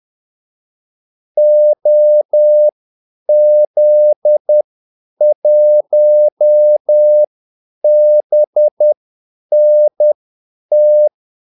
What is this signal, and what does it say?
Morse code, heard as a keyed tone, about 10 wpm, spaced as for 8 wpm, 600 Hz. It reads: OZ1BNT
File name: Callsigns-10-8wpm